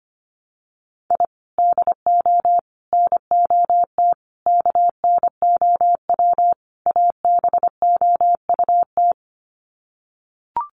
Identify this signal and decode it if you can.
Morse code, heard as a keyed tone, about 25 wpm, 700 Hz. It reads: I DO NOT KNOW ABOUT
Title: Sets of 5 Words 25wpm